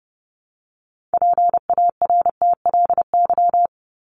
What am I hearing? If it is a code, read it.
Morse code, heard as a keyed tone, about 30 wpm, 700 Hz. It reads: PARTLY
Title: LY Suffix - Top 100 Words with Base Words 30wpm